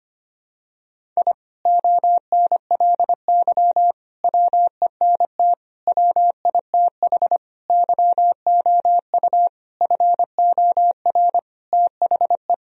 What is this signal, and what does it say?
Morse code, heard as a keyed tone, about 25 wpm, 700 Hz. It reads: I ONLY WENT WITH YOU FOR THE
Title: Sentences from Top 1000 Words - Review Part 1 - 25wpm